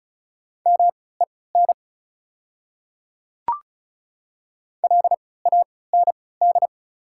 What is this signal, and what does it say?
Morse code, heard as a keyed tone, about 34 wpm, spaced as for 20 wpm, 700 Hz. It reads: MEN LAND
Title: 3x Character Spacing - Top 300 Words - 35wpm